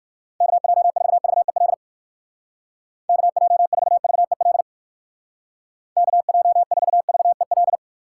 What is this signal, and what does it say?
Morse code, heard as a keyed tone, about 57 wpm, 700 Hz. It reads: KJ4VEL KJ4VEL KJ4VEL